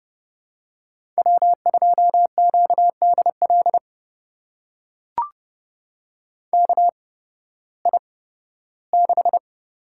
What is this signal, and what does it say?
Morse code, heard as a keyed tone, about 30 wpm, spaced as for 16 wpm, 700 Hz. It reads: W2QDL KI6
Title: USA Call Signs - 8x Character Spacing - 30wpm